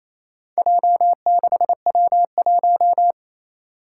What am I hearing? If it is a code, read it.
Morse code, heard as a keyed tone, about 29 wpm, 700 Hz. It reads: J6W1
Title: Four Challenging Letter-Number Combinations 28wpm